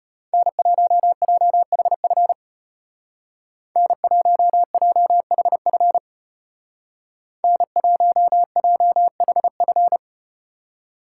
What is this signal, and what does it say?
Morse code, heard as a keyed tone, about 35 wpm, 700 Hz. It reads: N1JHF N1JHF N1JHF